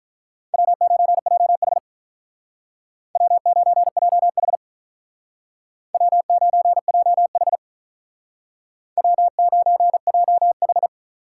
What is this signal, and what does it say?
Morse code, heard as a keyed tone, about 45 wpm, 700 Hz. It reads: W9JH W9JH W9JH W9JH